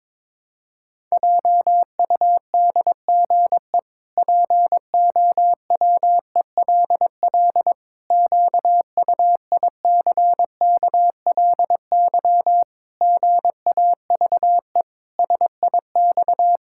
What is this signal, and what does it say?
Morse code, heard as a keyed tone, about 22 wpm, 700 Hz. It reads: JUDGE POWELL QUICKLY GAVE SIX
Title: Pangram Challenge v2 - 22wpm